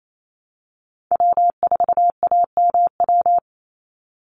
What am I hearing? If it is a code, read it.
Morse code, heard as a keyed tone, about 28 wpm, 700 Hz. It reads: W4AMW
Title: NA SOTA Chaser Callsigns 28wpm